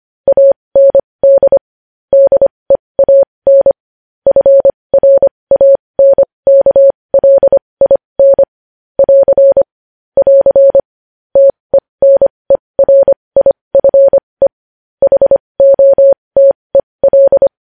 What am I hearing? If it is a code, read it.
Morse code, heard as a keyed tone, about 25 wpm, spaced as for 20 wpm, 550 Hz. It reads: AND DEAN FRANKLIN <AR> <AR> TENERIFE HOTEL
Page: News Headlines in Morse Code at 20 WPM